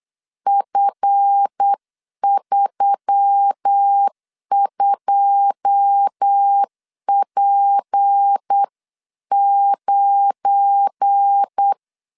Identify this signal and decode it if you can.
Morse code, heard as a keyed tone, about 8 wpm, 800 Hz. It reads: F32P9
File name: LEZIONI / 10 SIGNS / 10A SLASH 8wpm 3452